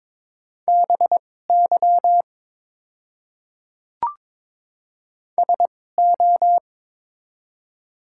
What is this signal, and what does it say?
Morse code, heard as a keyed tone, about 22 wpm, spaced as for 16 wpm, 700 Hz. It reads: BY SO
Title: 2x Character Spacing - Top 100 Words - 22wpm